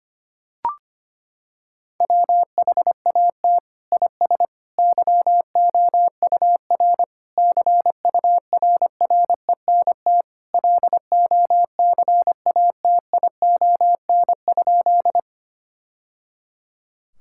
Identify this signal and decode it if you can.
Morse code, heard as a keyed tone, about 25 wpm, 700 Hz. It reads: WHAT IS YOUR CURRENT LOCATION?
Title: Sentences from Top 1000 Words - Review Part 6 - 25wpm